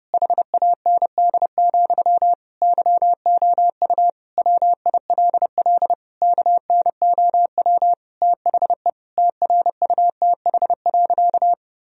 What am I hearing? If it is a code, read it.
Morse code, heard as a keyed tone, about 30 wpm, 700 Hz. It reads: HAND, YOU WILL KNOW THE TRUTH.
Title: Sentences from Top 1000 Words - Review Part 3 - 30wpm